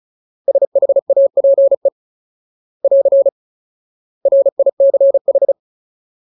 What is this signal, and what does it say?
Morse code, heard as a keyed tone, about 34 wpm, 550 Hz. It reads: SHAPE <AR> RICH
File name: Ultimate Word List / 35WPM Ultimate